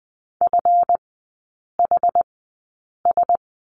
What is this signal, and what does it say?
Morse code, heard as a keyed tone, about 20 wpm, spaced as for 8 wpm, 700 Hz. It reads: FHS
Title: 138 Lesson 23e - Letter f - Three Characters 20wpm